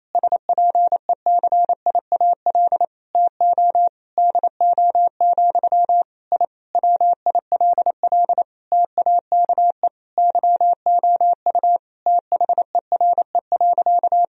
Morse code, heard as a keyed tone, about 28 wpm, 700 Hz. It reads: SPECIAL TO DO, I WILL TAKE YOU THERE.
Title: Sentences from Top 1000 Words - Review Part 3 - 28wpm